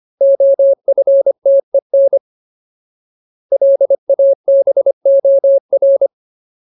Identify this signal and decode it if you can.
Morse code, heard as a keyed tone, about 25 wpm, 550 Hz. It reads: OFTEN LABOR
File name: Ultimate Word List / 25WPM Ultimate